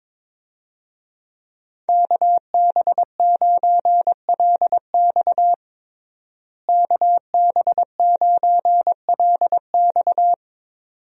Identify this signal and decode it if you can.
Morse code, heard as a keyed tone, about 21 wpm, 700 Hz. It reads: KB9LX KB9LX
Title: USA Call Signs - 5 Characters - Repeated 2x - 22wpm